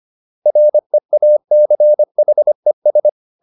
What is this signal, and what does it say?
Morse code, heard as a keyed tone, about 25 wpm, 600 Hz. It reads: REACHES